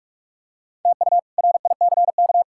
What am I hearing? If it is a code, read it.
Morse code, heard as a keyed tone, about 47 wpm, 700 Hz. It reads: TU RICK
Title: K1USN SST QSO Practice - 45wpm